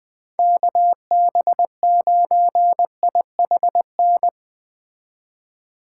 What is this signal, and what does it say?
Morse code, heard as a keyed tone, about 20 wpm, 700 Hz. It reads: KB9IHN